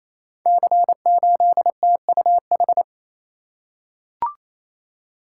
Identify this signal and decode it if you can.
Morse code, heard as a keyed tone, about 28 wpm, 700 Hz. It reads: C8TUH
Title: Code Groups - 5 Letter:Numbers - 28wpm